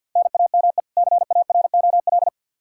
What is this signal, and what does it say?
Morse code, heard as a keyed tone, about 47 wpm, 700 Hz. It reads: NAME CAROL
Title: Practice QSO - Exchange 1 - Beginning - 50wpm